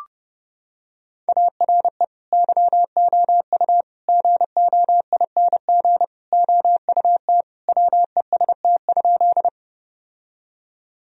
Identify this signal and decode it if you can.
Morse code, heard as a keyed tone, about 30 wpm, 700 Hz. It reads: ARE YOU GOING OUT WEST?
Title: Sentences from Top 1000 Words - Review Part 2 - 30wpm